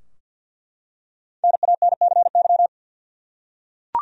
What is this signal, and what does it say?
Morse code, heard as a keyed tone, about 46 wpm, 700 Hz. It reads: NANCY
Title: USA Common Names - Max 5 Letters - 50wpm